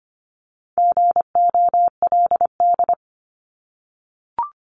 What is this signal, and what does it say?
Morse code, heard as a keyed tone, about 25 wpm, 700 Hz. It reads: GOLD
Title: Top 500 Words 25wpm